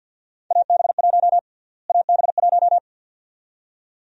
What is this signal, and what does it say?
Morse code, heard as a keyed tone, about 49 wpm, 700 Hz. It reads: AB1 AB1
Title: USA Call Sign Prefixes - Repeated 2x - 50wpm